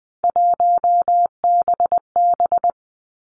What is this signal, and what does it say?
Morse code, heard as a keyed tone, about 20 wpm, 700 Hz. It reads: 1BB
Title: Three Challenging Letter-Number Combinations 20wpm